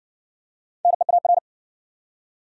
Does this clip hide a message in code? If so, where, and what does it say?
Yes: Morse code from 0.8 to 1.4 s, heard as a keyed tone, about 59 wpm, 700 Hz. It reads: NEAR